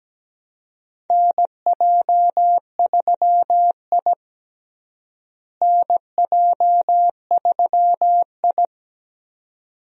Morse code, heard as a keyed tone, about 17 wpm, 700 Hz. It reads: NJ3I NJ3I
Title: USA Call Signs - 4 Characters - Repeated 2x - 17wpm